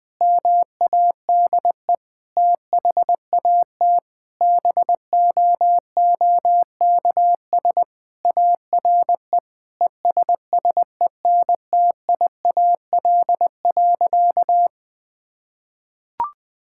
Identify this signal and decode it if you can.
Morse code, heard as a keyed tone, about 20 wpm, 700 Hz. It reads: MADE THAT BOOKS ARE ESSENTIAL.
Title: Sentences with 8-Word Phrases - 20wpm